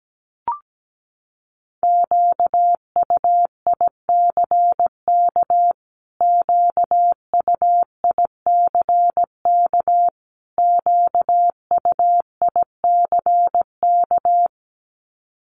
Morse code, heard as a keyed tone, about 17 wpm, 700 Hz. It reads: QUICK QUICK QUICK
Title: Top 500 Words Repeated 17wpm